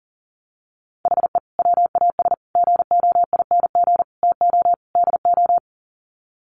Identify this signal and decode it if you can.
Morse code, heard as a keyed tone, about 40 wpm, 700 Hz. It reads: HE WAS GOING TO DO
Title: Sets of 5 Words Encore2 40wpm